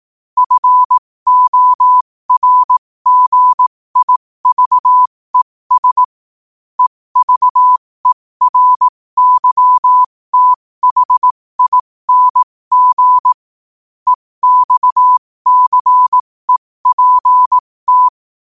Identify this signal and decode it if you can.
Morse code, heard as a keyed tone, about 18 wpm, spaced as for 15 wpm, 1000 Hz. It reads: FORGIVES EVERYTHING EXCEPT
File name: qotdfiles / 15wpm